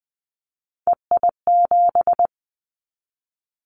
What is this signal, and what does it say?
Morse code, heard as a keyed tone, about 20 wpm, 700 Hz. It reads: EI7
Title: Callsign Prefixes for Top 50 DXCC - 20wpm